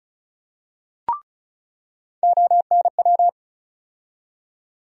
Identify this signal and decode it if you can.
Morse code, heard as a keyed tone, about 36 wpm, 700 Hz. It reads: ONW
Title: USA Call Sign Postfixes - 3 Letters 35wpm